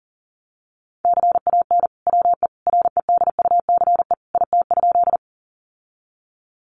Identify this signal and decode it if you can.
Morse code, heard as a keyed tone, about 40 wpm, 700 Hz. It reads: CAN WE REDUCE IT?
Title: Sentences from Top 1100 Words 40wpm